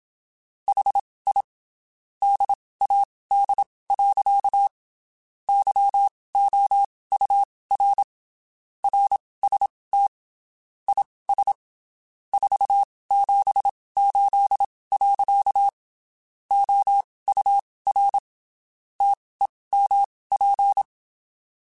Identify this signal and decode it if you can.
Morse code, heard as a keyed tone, about 27 wpm, spaced as for 19 wpm, 800 Hz. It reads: HI DAD. YOUR RST IS 478. OUR TEMP
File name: (spaced) / QSOs Spaced-mp3s / WD0FBT @25 sp